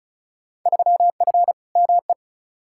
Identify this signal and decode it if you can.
Morse code, heard as a keyed tone, about 34 wpm, 700 Hz. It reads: 3F ME
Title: Field Day Exchanges - 35wpm